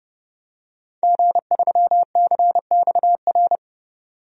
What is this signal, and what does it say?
Morse code, heard as a keyed tone, about 29 wpm, 700 Hz. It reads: G3CXR
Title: DXCC-100 Call Signs - 5 Characters - 30wpm